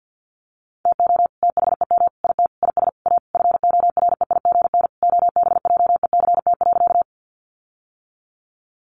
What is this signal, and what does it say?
Morse code, heard as a keyed tone, about 50 wpm, 700 Hz. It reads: TO THEM IT IS A FOREIGN OBJECT.
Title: Sentences from Top 1400 Words 50wpm